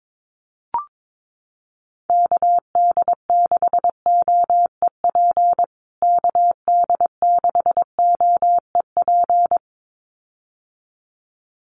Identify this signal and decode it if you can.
Morse code, heard as a keyed tone, about 22 wpm, 700 Hz. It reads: KD6OEP KD6OEP
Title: USA Call Signs Repeated 2x 22wpm